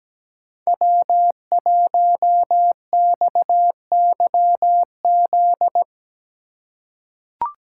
Morse code, heard as a keyed tone, about 17 wpm, 700 Hz. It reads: W1XYZ
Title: USA Call Signs - 5 Characters - 2x Character Spacing - 17wpm